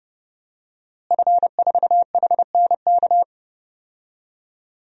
Morse code, heard as a keyed tone, about 30 wpm, 700 Hz. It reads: F4HNK
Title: DXCC-100 Call Signs - 5 Characters - Repeated 2x - 30wpm